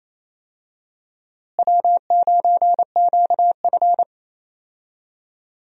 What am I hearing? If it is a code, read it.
Morse code, heard as a keyed tone, about 28 wpm, 700 Hz. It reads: W9QF
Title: USA Call Signs - 4 Characters - Repeated 3x - 28wpm